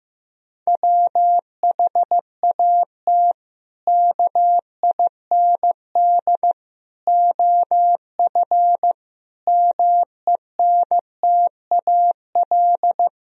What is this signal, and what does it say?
Morse code, heard as a keyed tone, about 15 wpm, 700 Hz. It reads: WHAT KIND OF MENTAL
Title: Sentences from Top 1000 Words - Review Part 4 - 15wpm